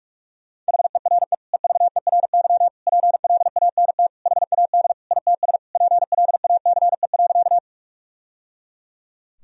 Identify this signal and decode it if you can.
Morse code, heard as a keyed tone, about 41 wpm, 700 Hz. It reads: HERE EVERY PLANT HAD ITS PLACE.
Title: Sentences from Top 1000 Words - Review Part 1 - 45wpm